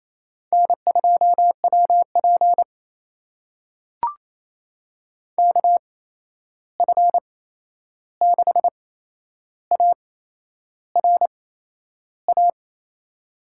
Morse code, heard as a keyed tone, about 28 wpm, spaced as for 11 wpm, 700 Hz. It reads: N2WP KF6ARA
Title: USA Call Signs - 8x Character Spacing - 28wpm